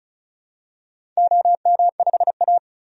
Morse code, heard as a keyed tone, about 36 wpm, 690 Hz. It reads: OM5A
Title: DXCC-100 Call Signs - 4 Characters - 35wpm